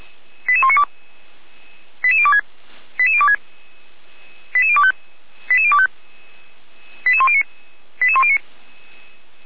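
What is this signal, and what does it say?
Five-tone selective call (ZVEI), heard as quick runs of five tones, tones of about 70 ms. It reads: 88192 88026 88026 88025 88025 88109 88109
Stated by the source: Five-tone ZVEI selective call received on SDR, recorded by feuerwehr-orlamuende.org (fire brigade Orlamünde).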